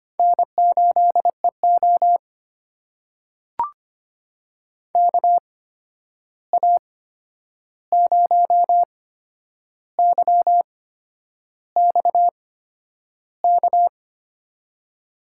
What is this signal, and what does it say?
Morse code, heard as a keyed tone, about 25 wpm, spaced as for 10 wpm, 700 Hz. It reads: N8EO KA0YX K
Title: USA Call Signs - 8x Character Spacing - 25wpm